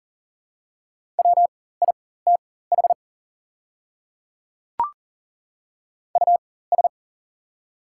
Morse code, heard as a keyed tone, about 40 wpm, spaced as for 19 wpm, 700 Hz. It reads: WITH US
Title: 4x Character Spacing - Top 300 Words - 40wpm